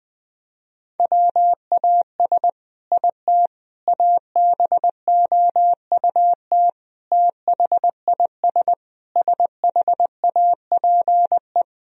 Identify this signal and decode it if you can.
Morse code, heard as a keyed tone, about 20 wpm, 700 Hz. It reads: WAS IT ABOUT THIS SHAPE
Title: Sentences from Top 1000 Words - Review Part 3 - 20wpm